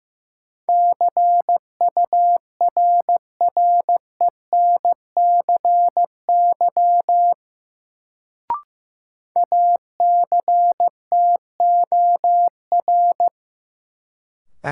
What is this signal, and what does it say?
Morse code, heard as a keyed tone, about 15 wpm, 700 Hz. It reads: CURRENCY ACTOR
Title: Sentences from Top 1500 Words - New Word Drill - 15wpm